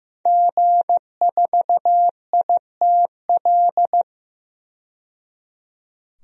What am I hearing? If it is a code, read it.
Morse code, heard as a keyed tone, about 15 wpm, 700 Hz. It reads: G4ITL
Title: DXCC-100 Call Signs - 5 Characters - 15wpm